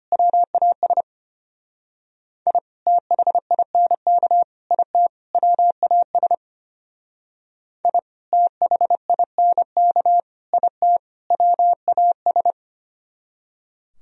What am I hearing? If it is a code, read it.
Morse code, heard as a keyed tone, about 32 wpm, 700 Hz. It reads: WAS I THINK IT WAS I THINK IT WAS